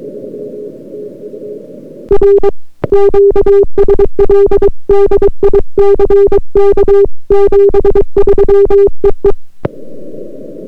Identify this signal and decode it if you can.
Morse code, heard as a keyed tone, about 22 wpm, 400 Hz. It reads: R QSL DICK 73 EE